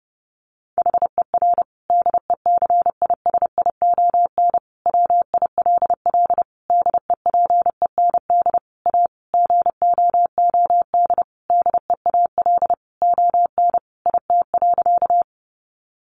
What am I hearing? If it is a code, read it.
Morse code, heard as a keyed tone, about 30 wpm, 700 Hz. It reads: HER DECISION WILL DEPEND A GOOD DEAL ON IT.
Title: Sentences from Top 1300 Words 30wpm